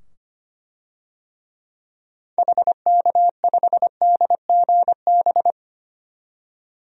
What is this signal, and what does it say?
Morse code, heard as a keyed tone, about 25 wpm, 700 Hz. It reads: HK5DGB
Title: DXCC-100 Call Signs - 6 Characters - 25wpm